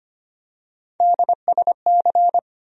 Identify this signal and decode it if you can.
Morse code, heard as a keyed tone, about 25 wpm, 700 Hz. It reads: DSC
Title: USA Call Sign Postfixes - 3 Letters 25wpm